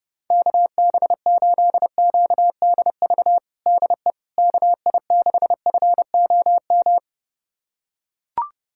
Morse code, heard as a keyed tone, about 32 wpm, 700 Hz. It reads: KB8QDV DE KI6FOM
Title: USA Callsign DE Callsign - 6 Characters - 30wpm